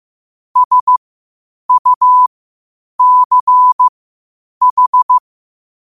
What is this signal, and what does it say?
Morse code, heard as a keyed tone, about 15 wpm, spaced as for 8 wpm, 1000 Hz. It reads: SUCH
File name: qotdfiles / 8wpm